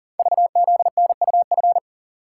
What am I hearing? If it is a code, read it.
Morse code, heard as a keyed tone, about 40 wpm, 700 Hz. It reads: VZNUF
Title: Code Groups - 5 Letter:Numbers - 40wpm